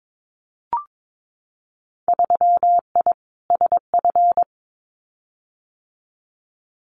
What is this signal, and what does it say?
Morse code, heard as a keyed tone, about 22 wpm, 700 Hz. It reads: 3I SF
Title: Winter Field Day Exchanges 22wpm